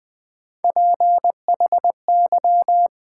Morse code, heard as a keyed tone, about 20 wpm, 700 Hz. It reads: PHY
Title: Build A Word - 3 Letters At A Time - 8-Letter Words - 20wpm